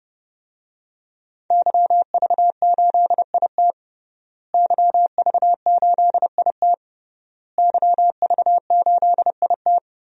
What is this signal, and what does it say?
Morse code, heard as a keyed tone, about 30 wpm, 700 Hz. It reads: YV8IT YV8IT YV8IT
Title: DXCC-100 Call Signs - 5 Characters - Repeated 3x - 30wpm